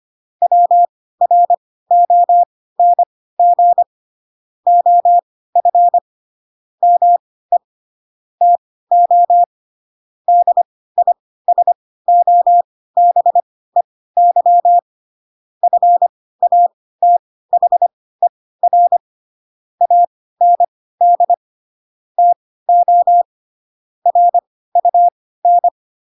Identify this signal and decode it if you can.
Morse code, heard as a keyed tone, about 25 wpm, spaced as for 16 wpm, 700 Hz. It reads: WRONG OF ME TO DISOBEY FATHER AND TO RUN